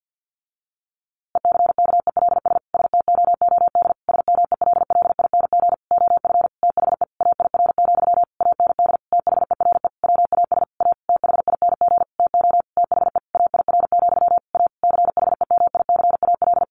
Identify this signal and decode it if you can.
Morse code, heard as a keyed tone, about 50 wpm, 700 Hz. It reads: ECKELS STOOD SMELLING OF THE AIR, AND THERE WAS A THING TO THE AIR, A CHEMICAL
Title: A Sound Of Thunder 50wpm